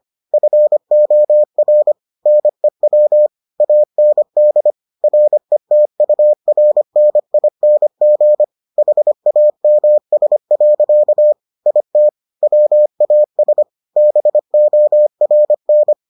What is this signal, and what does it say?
Morse code, heard as a keyed tone, about 25 wpm, 600 Hz. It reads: FOR NEW AND RETURNING HAMS. IT WAS BORN